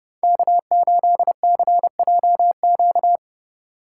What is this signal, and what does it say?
Morse code, heard as a keyed tone, about 31 wpm, 700 Hz. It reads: K8CJQ